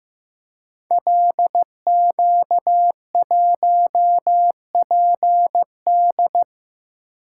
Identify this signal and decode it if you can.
Morse code, heard as a keyed tone, about 15 wpm, 700 Hz. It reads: LQ1PD
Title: Code Groups - 5 Letter:Numbers - 15wpm